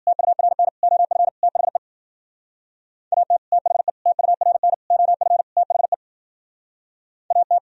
Morse code, heard as a keyed tone, about 49 wpm, 700 Hz. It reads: TURN OF THE AT THE TURN OF THE AT
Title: Speed-Racing - Sets of 5 Words 40wpm